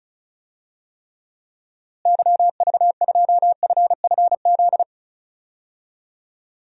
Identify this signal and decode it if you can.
Morse code, heard as a keyed tone, about 36 wpm, 700 Hz. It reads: YV2FFZ
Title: DXCC-100 Call Signs - 6 Characters - Repeated 3x - 35wpm